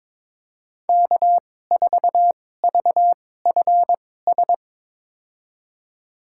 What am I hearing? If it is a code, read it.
Morse code, heard as a keyed tone, about 22 wpm, spaced as for 16 wpm, 700 Hz. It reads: K4VFS
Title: USA Call Signs - 5 Characters - 2x Character Spacing - 22wpm